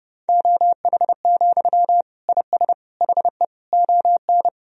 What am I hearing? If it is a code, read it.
Morse code, heard as a keyed tone, about 30 wpm, 700 Hz. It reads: OH, IS HE ON
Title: Sentences from Top 1000 Words - Review Part 1 - 30wpm